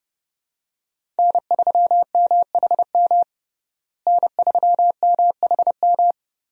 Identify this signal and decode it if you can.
Morse code, heard as a keyed tone, about 30 wpm, 700 Hz. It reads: N3MHM N3MHM
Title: USA Call Signs - 5 Characters - Repeated 3x - 30wpm